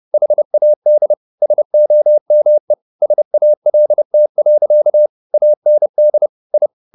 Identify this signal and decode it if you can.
Morse code, heard as a keyed tone, about 30 wpm, 600 Hz. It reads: HAD SOME SALT. AND I
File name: the old man and the sea / the old man and the sea 1 30wpm